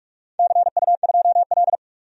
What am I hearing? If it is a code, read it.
Morse code, heard as a keyed tone, about 44 wpm, 700 Hz. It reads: CU2L
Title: DXCC-100 Call Signs - 4 Characters - 45wpm